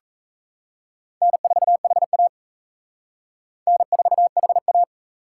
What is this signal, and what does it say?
Morse code, heard as a keyed tone, about 34 wpm, 700 Hz. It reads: N4HA N4HA
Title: Speed Racing - USA Call Signs 28wpm